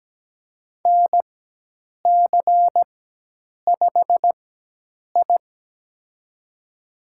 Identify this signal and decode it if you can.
Morse code, heard as a keyed tone, about 17 wpm, spaced as for 8 wpm, 700 Hz. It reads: NC5I
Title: USA Call Signs - 4x Character Spacing - 17wpm